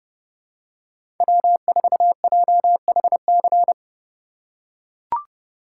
Morse code, heard as a keyed tone, about 31 wpm, 700 Hz. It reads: W4JHC
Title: USA Call Signs - 5 Characters - 4x Character Spacing - 30wpm